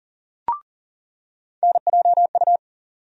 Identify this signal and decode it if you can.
Morse code, heard as a keyed tone, about 40 wpm, 700 Hz. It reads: NJU
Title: USA Call Sign Postfixes - 3 Letters 40wpm